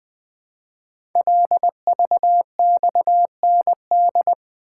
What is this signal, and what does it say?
Morse code, heard as a keyed tone, about 20 wpm, 700 Hz. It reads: LVXND
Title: Code Groups - 5 Letters - 20wpm